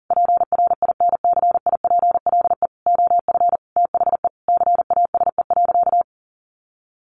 Morse code, heard as a keyed tone, about 40 wpm, 700 Hz. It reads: PRINCIPLE OF THE CASE.
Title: Sentences from Top 1400 Words 40wpm